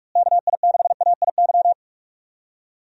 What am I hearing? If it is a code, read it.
Morse code, heard as a keyed tone, about 47 wpm, 700 Hz. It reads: KI6AIY
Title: USA Call Signs - 6 Characters - Repeated 3x - 45wpm